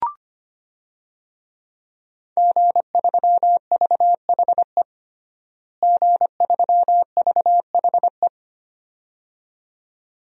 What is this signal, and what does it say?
Morse code, heard as a keyed tone, about 25 wpm, 700 Hz. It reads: G3VHE G3VHE
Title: DXCC-100 Call Signs - 5 Characters - Repeated 2x - 25wpm